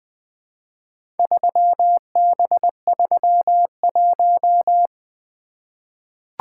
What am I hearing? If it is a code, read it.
Morse code, heard as a keyed tone, about 20 wpm, 700 Hz. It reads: 3B31
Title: Four Letter-Number Combinations - 20wpm